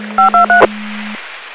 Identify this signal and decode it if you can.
DTMF, heard as telephone touch tones, tones of about 120 ms. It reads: 523